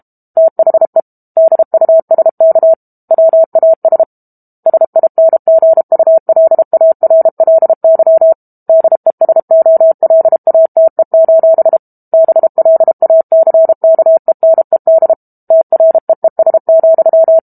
Morse code, heard as a keyed tone, about 32 wpm, 650 Hz. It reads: THE DUSK WAS SINGULARLY DESOLATE: BLACKENED TREES,
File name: Book Two / 06-Chapter 6 - 32 WPM